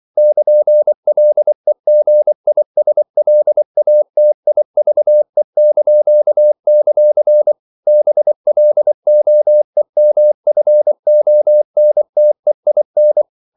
Morse code, heard as a keyed tone, about 24 wpm, 600 Hz. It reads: <KN>LEGISLATIVE); BLOEMFONTEIN
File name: CW / Practice / NationalCapitals24